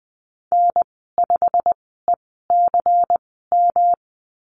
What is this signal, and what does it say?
Morse code, heard as a keyed tone, about 20 wpm, spaced as for 14 wpm, 700 Hz. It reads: N5ECM
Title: 161 Lesson 26g - Letter g - Call Signs 20wpm